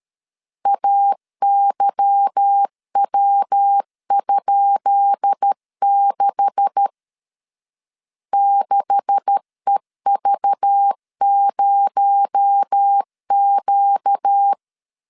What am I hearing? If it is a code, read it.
Morse code, heard as a keyed tone, about 13 wpm, 800 Hz. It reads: AYW?6 6EV0Q
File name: LEZIONI / 10 SIGNS / 10C DOT 12wpm 2451